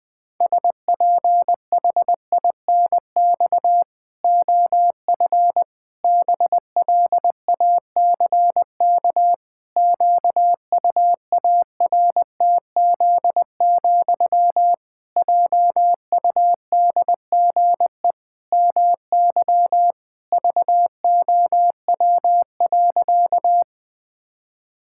Morse code, heard as a keyed tone, about 20 wpm, 700 Hz. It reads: SPHINX OF BLACK QUARTZ, JUDGE MY VOW.
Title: Pangram Challenge v2 - 20wpm